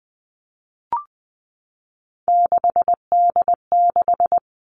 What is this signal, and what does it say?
Morse code, heard as a keyed tone, about 20 wpm, 700 Hz. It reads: 6D6
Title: Three Challenging Letter-Number Combinations 20wpm